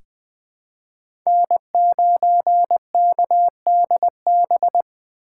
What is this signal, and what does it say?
Morse code, heard as a keyed tone, about 20 wpm, 700 Hz. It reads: N9KDB
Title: K1USN SST QSO Practice - 20wpm